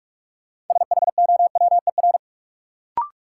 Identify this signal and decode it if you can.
Morse code, heard as a keyed tone, about 47 wpm, 700 Hz. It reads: SHOWER
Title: Sentences from Top 1600 Words - New Word Drill - 45wpm